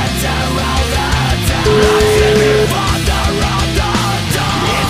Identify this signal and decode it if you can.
Ringback tone (Europe, 425 Hz), with noise.